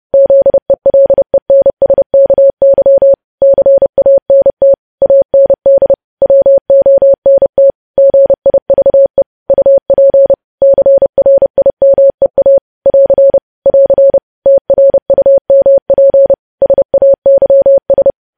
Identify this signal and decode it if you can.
Morse code, heard as a keyed tone, about 30 wpm, 550 Hz. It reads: ZELENSKY CANT AND WONT GIVE UP CRIMEA <AR> <AR> TRUMP SAYS
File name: podcast / News headlines-1x-30x30